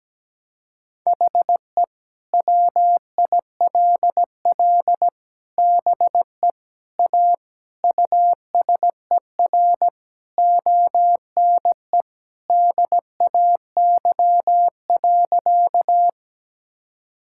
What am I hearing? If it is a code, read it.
Morse code, heard as a keyed tone, about 17 wpm, 700 Hz. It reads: HE WILL BE A USER ONE DAY.
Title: Sentences from Top 1000 Words - Review Part 5 - 17wpm